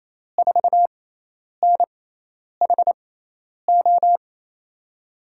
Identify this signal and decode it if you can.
Morse code, heard as a keyed tone, about 28 wpm, spaced as for 10 wpm, 700 Hz. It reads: 4NHO
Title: USA Call Signs - 5 Characters - 6x Character Spacing - 28wpm